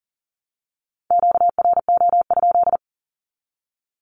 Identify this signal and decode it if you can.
Morse code, heard as a keyed tone, about 40 wpm, 700 Hz. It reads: QRO?
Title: Q-Codes 40wpm